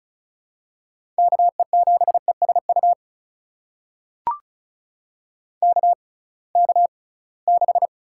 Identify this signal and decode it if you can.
Morse code, heard as a keyed tone, about 36 wpm, spaced as for 23 wpm, 700 Hz. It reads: KE7ESU KK6
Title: USA Call Signs - 6 Characters - 6x Character Spacing - 35wpm